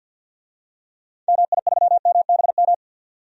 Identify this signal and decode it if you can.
Morse code, heard as a keyed tone, about 52 wpm, 700 Hz. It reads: MI3MBM
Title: DXCC-100 Call Signs - 6 Characters - 50wpm